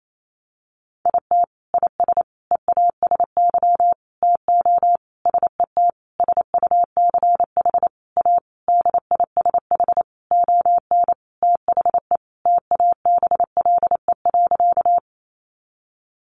Morse code, heard as a keyed tone, about 28 wpm, 700 Hz. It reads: IT IS EASY TO SET SUCH A DISH ON THE TABLE.
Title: Sentences from Top 1300 Words 28wpm